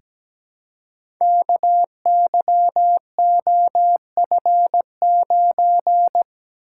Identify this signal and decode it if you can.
Morse code, heard as a keyed tone, about 17 wpm, 700 Hz. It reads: KYOF9
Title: Code Groups - 5 Letter:Numbers - 17wpm